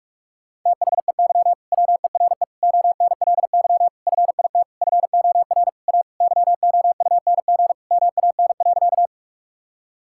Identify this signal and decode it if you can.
Morse code, heard as a keyed tone, about 41 wpm, 700 Hz. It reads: THEY WERE ONLY FIT FOR A YOUNG MAN.
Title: Sentences from Top 1000 Words - Review Part 5 - 45wpm